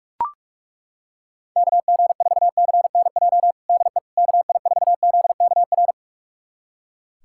Morse code, heard as a keyed tone, about 41 wpm, 700 Hz. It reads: KG4CNJ DE KI4ZKR
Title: USA Callsign DE Callsign - 6 Characters - 45wpm